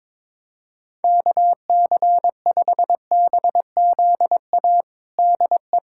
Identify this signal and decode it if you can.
Morse code, heard as a keyed tone, about 22 wpm, 700 Hz. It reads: KC5BZA DE
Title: USA Callsign DE Callsign 22wpm